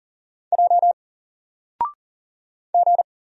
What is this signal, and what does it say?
Morse code, heard as a keyed tone, about 40 wpm, 700 Hz. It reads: J G
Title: Single Letter-Number 40wpm